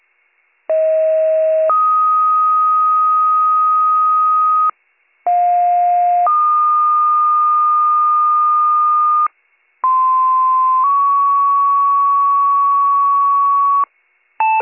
Two-tone page, heard as a one-second tone, then a three-second tone, about 640 then 1250 Hz, about 710 then 1220 Hz, about 1040 then 1120 Hz. It is heard from 0.7 to 13.8 s.